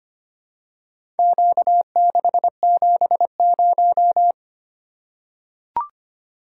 Morse code, heard as a keyed tone, about 25 wpm, 700 Hz. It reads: Q670
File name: practice / 25wpm / Four Letter-Number Combinations - 25wpm